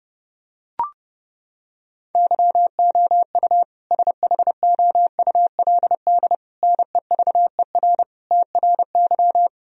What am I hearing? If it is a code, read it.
Morse code, heard as a keyed tone, about 32 wpm, 700 Hz. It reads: YOU SHOULD NEVER TRY
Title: Sentences from Top 1000 Words - Review Part 3 - 30wpm